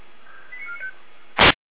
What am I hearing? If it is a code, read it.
Five-tone selective call (ZVEI), heard as quick runs of five tones, tones of about 90 ms. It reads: 89495